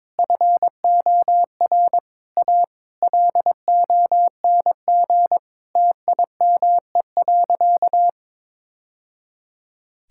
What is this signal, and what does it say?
Morse code, heard as a keyed tone, about 21 wpm, 700 Hz. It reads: FOR A LONG TIME.
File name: practice / 22wpm / Sentences with 3-Word Phrases - 22wpm